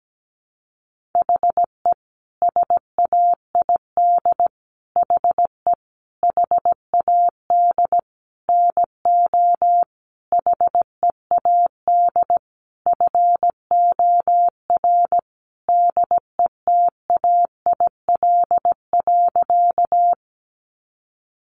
Morse code, heard as a keyed tone, about 17 wpm, 700 Hz. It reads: HE SAID HE HAD NO HEAD FOR DETAIL.
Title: Sentences from Top 1500 Words 17wpm